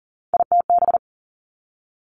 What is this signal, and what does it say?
Morse code, heard as a keyed tone, about 40 wpm, 700 Hz. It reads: ITB
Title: Three Challenging Letter-Number Combinations 40wpm